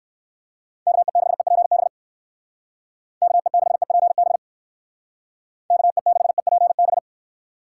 Morse code, heard as a keyed tone, about 58 wpm, 700 Hz. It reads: KE6EPB KE6EPB KE6EPB